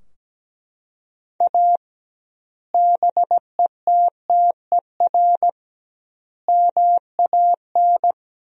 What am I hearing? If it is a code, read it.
Morse code, heard as a keyed tone, about 17 wpm, 700 Hz. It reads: A BETTER MAN
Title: 2x Word-Spacing - Sentences from Top 400 Words - 17wpm